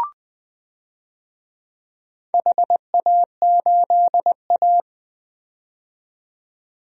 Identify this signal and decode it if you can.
Morse code, heard as a keyed tone, about 20 wpm, 700 Hz. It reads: HA8A
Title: DXCC-100 Call Signs - 4 Characters - 20wpm